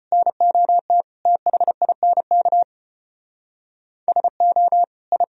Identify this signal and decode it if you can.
Morse code, heard as a keyed tone, about 32 wpm, 700 Hz. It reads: NOT THINK SO I